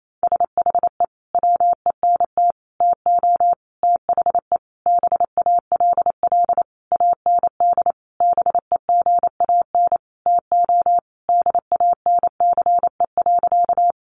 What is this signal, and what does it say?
Morse code, heard as a keyed tone, about 28 wpm, 700 Hz. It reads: SHE WENT TO THE BALL AND BEGAN TO DANCE.
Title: Sentences from Top 900 Words 28wpm